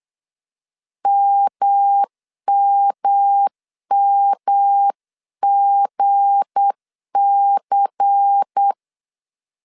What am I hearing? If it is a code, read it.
Morse code, heard as a keyed tone, about 8 wpm, 800 Hz. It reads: MMMGC